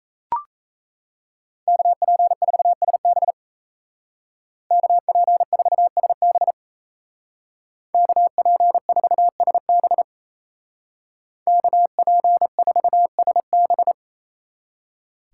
Morse code, heard as a keyed tone, about 33 wpm, 700 Hz. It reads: KP4SB KP4SB KP4SB KP4SB
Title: Speed Racing - USA 2x2 Call Signs 28wpm